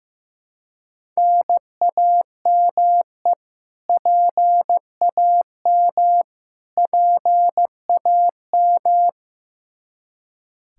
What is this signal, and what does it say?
Morse code, heard as a keyed tone, about 15 wpm, 700 Hz. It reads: NAME PAM PAM
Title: Practice QSO - Exchange 1 - Beginning - 15wpm